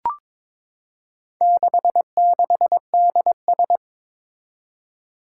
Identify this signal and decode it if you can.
Morse code, heard as a keyed tone, about 22 wpm, 700 Hz. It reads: 66DS
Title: Four Challenging Letter-Number Combinations 22wpm